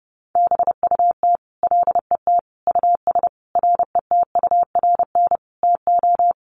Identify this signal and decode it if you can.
Morse code, heard as a keyed tone, about 30 wpm, 700 Hz. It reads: BUT LET US RETURN TO
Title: Sentences from Top 800 Words 30wpm